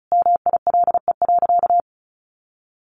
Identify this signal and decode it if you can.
Morse code, heard as a keyed tone, about 34 wpm, 700 Hz. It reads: MILE.
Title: Sentences from Top 1300 Words 35wpm